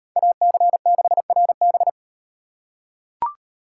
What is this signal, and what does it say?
Morse code, heard as a keyed tone, about 35 wpm, 700 Hz. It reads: AC6RB